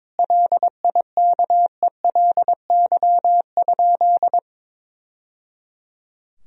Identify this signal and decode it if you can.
Morse code, heard as a keyed tone, about 22 wpm, 700 Hz. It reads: LIKELY?
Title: Sentences from Top 1000 Words - Review Part 4 - 22wpm